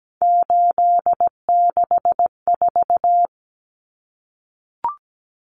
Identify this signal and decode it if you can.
Morse code, heard as a keyed tone, about 17 wpm, 700 Hz. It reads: 864
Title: Common Cut Numbers - 3-Digit 17wpm